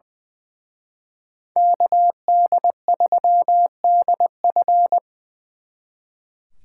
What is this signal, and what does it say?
Morse code, heard as a keyed tone, about 24 wpm, 700 Hz. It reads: KD3DF
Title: Speed Racing - USA 2x2 Call Signs 20wpm